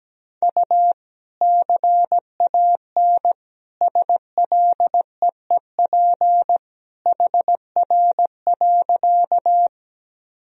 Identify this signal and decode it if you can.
Morse code, heard as a keyed tone, about 17 wpm, 700 Hz. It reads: U CAN SLEEP HR.
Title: Sentences from Top 500 Words and Abbreviations 17wpm